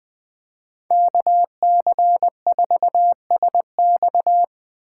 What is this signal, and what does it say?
Morse code, heard as a keyed tone, about 20 wpm, 700 Hz. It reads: KC4SX
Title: Speed Racing - USA Call Signs 17wpm